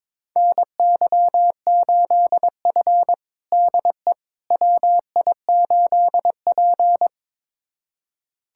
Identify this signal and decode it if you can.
Morse code, heard as a keyed tone, about 22 wpm, 700 Hz. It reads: NY8F DE WI8P
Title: USA Callsign DE Callsign - 4 Characters - 22wpm